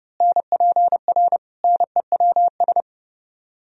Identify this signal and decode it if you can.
Morse code, heard as a keyed tone, about 29 wpm, 700 Hz. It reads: NPR NEWS
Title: Speed-Racing - Sets of 5 Words 30wpm